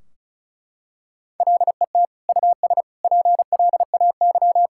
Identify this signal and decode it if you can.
Morse code, heard as a keyed tone, about 36 wpm, 700 Hz. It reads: LET US PLAY
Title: Sentences from Top 1600 Words 35wpm